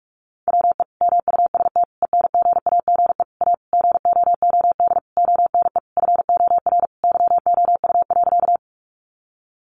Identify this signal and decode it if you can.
Morse code, heard as a keyed tone, about 44 wpm, 700 Hz. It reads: WE MUST ENGAGE A GOOD ONE FOR YOU.
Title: Sentences from Top 1500 Words 45wpm